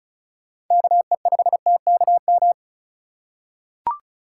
Morse code, heard as a keyed tone, about 36 wpm, 700 Hz. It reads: KE5TKM
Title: USA Call Signs - 6 Characters - 4x Character Spacing - 35wpm